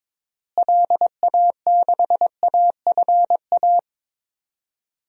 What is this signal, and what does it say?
Morse code, heard as a keyed tone, about 21 wpm, 700 Hz. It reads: LA6AFA
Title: DXCC-100 Call Signs - 6 Characters - Repeated 2x - 22wpm